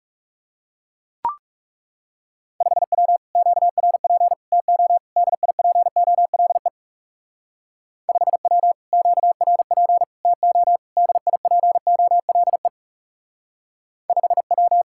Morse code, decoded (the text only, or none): 5W QRP TO DIPOLE 5W QRP TO DIPOLE 5W